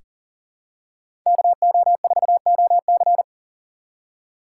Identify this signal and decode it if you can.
Morse code, heard as a keyed tone, about 40 wpm, 700 Hz. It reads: KO4OC
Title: USA Call Signs - 6x Character Spacing - 40wpm